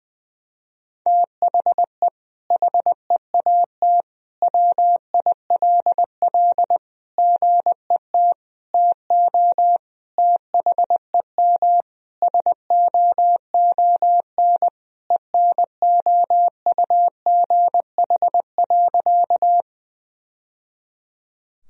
Morse code, heard as a keyed tone, about 19 wpm, 700 Hz. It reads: THE HEAT WILL GET TO THEM SOON ENOUGH.
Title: Sentences from Top 1000 Words - Review Part 3 - 20wpm